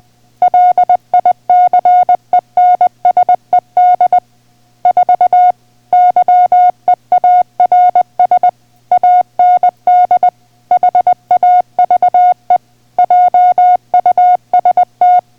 Morse code, heard as a keyed tone, about 20 wpm, 720 Hz. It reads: LICENSED 4 YEARS AND HAVE JUST